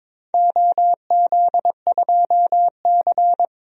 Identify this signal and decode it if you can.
Morse code, heard as a keyed tone, about 22 wpm, 700 Hz. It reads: OZ2C
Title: DXCC-100 Call Signs - 4 Characters - 22wpm